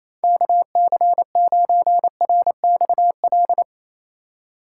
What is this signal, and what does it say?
Morse code, heard as a keyed tone, about 27 wpm, 700 Hz. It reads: KC9RXL
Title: USA Call Signs - 6 Characters - 3x Character Spacing - 28wpm